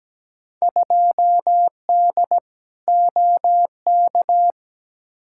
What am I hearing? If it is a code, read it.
Morse code, heard as a keyed tone, about 17 wpm, 700 Hz. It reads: 2D OK
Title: Field Day Exchanges Encore - 17wpm